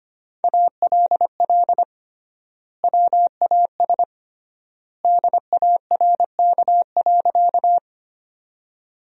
Morse code, heard as a keyed tone, about 25 wpm, 700 Hz. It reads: ALL WAS DARK.
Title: 3x Word-Spacing - Sentences from Top 500 Words - 25wpm